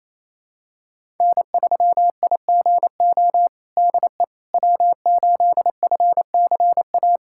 Morse code, heard as a keyed tone, about 27 wpm, 700 Hz. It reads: N3IGO DE W8FCA
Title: USA Callsign DE Callsign - 5 Characters - 28wpm